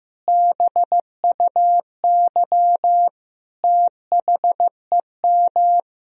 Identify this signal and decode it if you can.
Morse code, heard as a keyed tone, about 15 wpm, 700 Hz. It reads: BUY THEM
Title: Sentences from Top 1000 Words - Review Part 4 - 15wpm